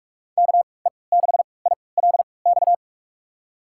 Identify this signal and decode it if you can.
Morse code, heard as a keyed tone, about 42 wpm, spaced as for 25 wpm, 700 Hz. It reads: KE6ILX
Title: USA Call Signs - 3x Character Spacing - 45wpm